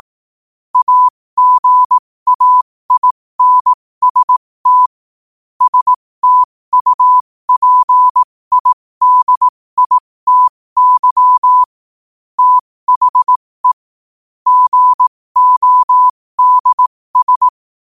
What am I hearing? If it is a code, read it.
Morse code, heard as a keyed tone, about 18 wpm, spaced as for 15 wpm, 1000 Hz. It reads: AGAINST STUPIDITY THE GODS
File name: qotdfiles / 15wpm